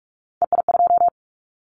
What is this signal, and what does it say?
Morse code, heard as a keyed tone, about 46 wpm, 700 Hz. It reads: EI2
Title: Callsign Prefixes for Top 50 DXCC - 45wpm